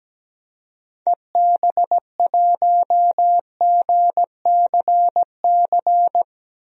Morse code, heard as a keyed tone, about 17 wpm, 700 Hz. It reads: EB1GCC
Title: DXCC-100 Call Signs - 6 Characters - 17wpm